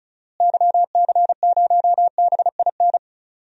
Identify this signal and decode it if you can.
Morse code, heard as a keyed tone, about 34 wpm, 700 Hz. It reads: YC0BIN
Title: DXCC-100 Call Signs - 6 Characters - Repeated 3x - 35wpm